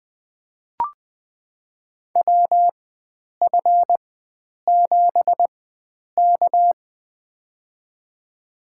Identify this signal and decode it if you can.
Morse code, heard as a keyed tone, about 20 wpm, spaced as for 9 wpm, 700 Hz. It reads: WF7K
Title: USA Call Signs - 4x Character Spacing - 20wpm